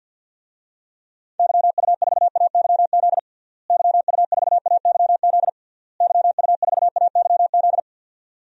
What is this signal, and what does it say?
Morse code, heard as a keyed tone, about 51 wpm, 700 Hz. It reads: YU4AYZ YU4AYZ YU4AYZ
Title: DXCC-100 Call Signs - 6 Characters - Repeated 3x - 50wpm